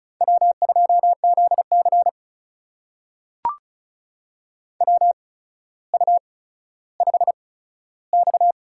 Morse code, heard as a keyed tone, about 36 wpm, spaced as for 15 wpm, 700 Hz. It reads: W2ZC WU5X
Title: USA Call Signs - 4 Characters - 8x Character Spacing - 35wpm